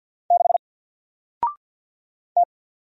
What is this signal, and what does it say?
Morse code, heard as a keyed tone, about 48 wpm, 700 Hz. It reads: 6 T
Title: Single Letter-Number Alt 50wpm